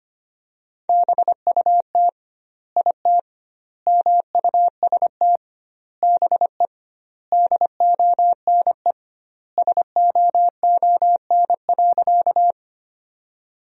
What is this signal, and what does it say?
Morse code, heard as a keyed tone, about 25 wpm, 700 Hz. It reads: BUT IT MUST BE DONE SOON.
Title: 2x Word-Spacing - Sentences from Top 500 Words - 25wpm